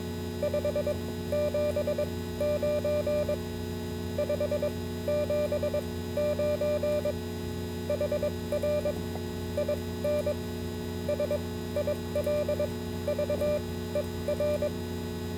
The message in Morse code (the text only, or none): 579 579 HR IN SILVER